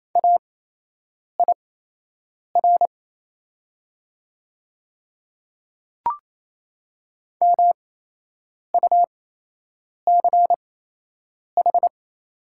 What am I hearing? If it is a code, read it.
Morse code, heard as a keyed tone, about 28 wpm, spaced as for 8 wpm, 690 Hz. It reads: AIR MUCH
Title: 8x Character Spacing - Top 300 Words - 28wpm